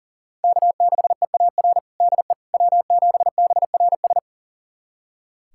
Morse code, heard as a keyed tone, about 40 wpm, 700 Hz. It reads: K6EAR DE W7BRS
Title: USA Callsign DE Callsign - 5 Characters - 40wpm